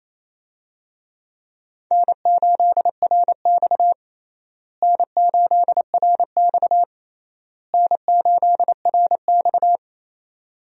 Morse code, heard as a keyed tone, about 28 wpm, 700 Hz. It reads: N8RX N8RX N8RX
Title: USA Call Signs - 4 Characters - Repeated 3x - 28wpm